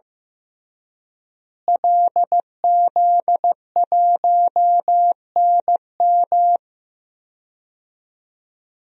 Morse code, heard as a keyed tone, about 16 wpm, 700 Hz. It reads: LZ1NM
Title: DXCC-100 Call Signs - 5 Characters - Repeated 3x - 15wpm